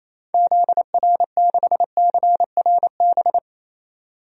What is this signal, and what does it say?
Morse code, heard as a keyed tone, about 28 wpm, 700 Hz. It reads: ZR6CRB
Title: DXCC-100 Call Signs - 6 Characters - 28wpm